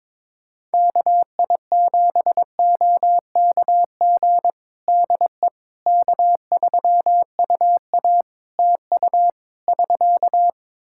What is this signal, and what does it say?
Morse code, heard as a keyed tone, about 21 wpm, 700 Hz. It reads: KI7OKG DE K3UA TU <SK>
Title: Practice QSO - Exchange 4 - Ending - 22wpm